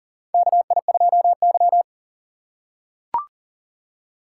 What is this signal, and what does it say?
Morse code, heard as a keyed tone, about 40 wpm, 690 Hz. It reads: KI2Y
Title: USA Call Signs - 4x Character Spacing - 40wpm